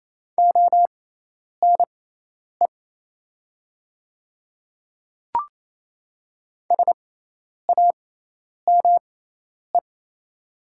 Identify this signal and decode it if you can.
Morse code, heard as a keyed tone, about 27 wpm, spaced as for 10 wpm, 700 Hz. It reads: ONE SAME
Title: 6x Character Spacing - Top 200 Words - 28wpm